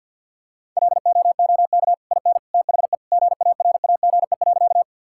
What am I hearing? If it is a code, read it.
Morse code, heard as a keyed tone, about 50 wpm, 700 Hz. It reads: LOOK IN THE GARAGE.
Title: Sentences from Top 1600 Words 50wpm